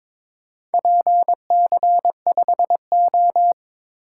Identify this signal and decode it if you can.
Morse code, heard as a keyed tone, about 22 wpm, 700 Hz. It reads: PC5O
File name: practice / 22wpm / Four Letter-Number Combinations - 22wpm